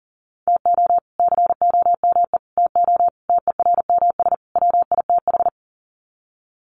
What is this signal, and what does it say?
Morse code, heard as a keyed tone, about 37 wpm, 700 Hz. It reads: TO COME TO TERMS WITH
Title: Sets of 5 Words Encore 40wpm